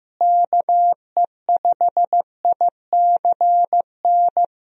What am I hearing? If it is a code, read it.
Morse code, heard as a keyed tone, about 15 wpm, 700 Hz. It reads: KE5ICN
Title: USA Call Signs - 6 Characters - Repeated 2x - 15wpm